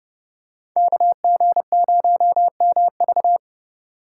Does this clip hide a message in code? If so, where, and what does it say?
Yes: Morse code from 0.8 to 3.4 s, heard as a keyed tone, about 29 wpm, 700 Hz. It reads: KG0MV